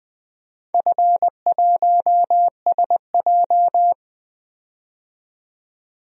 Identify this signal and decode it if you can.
Morse code, heard as a keyed tone, about 20 wpm, 700 Hz. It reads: F1SJ
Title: DXCC-100 Call Signs - 4 Characters - 20wpm